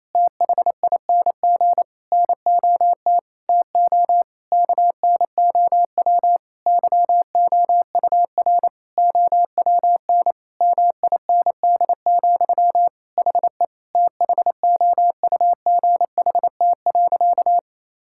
Morse code, heard as a keyed tone, about 28 wpm, 700 Hz. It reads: THING NOT TO KNOW YOUR OWN MIND, HE THOUGHT.
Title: Sentences from Top 1000 Words - Review Part 6 - 28wpm